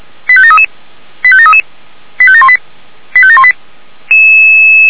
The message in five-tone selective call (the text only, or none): 85733 85733 85718 85718